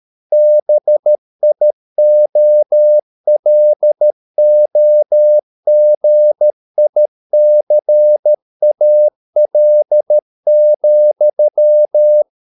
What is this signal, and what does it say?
Morse code, heard as a keyed tone, about 13 wpm, 600 Hz. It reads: BIOLOGICAL,